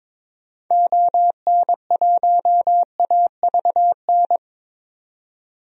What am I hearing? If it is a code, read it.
Morse code, heard as a keyed tone, about 22 wpm, 700 Hz. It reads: ON1AVN
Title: DXCC-100 Call Signs - 6 Characters - 22wpm